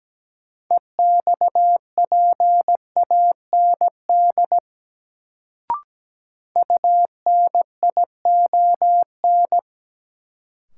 Morse code, heard as a keyed tone, about 17 wpm, 700 Hz. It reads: EXPAND UNION
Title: Sentences from Top 1400 Words - New Word Drill Alt - 17wpm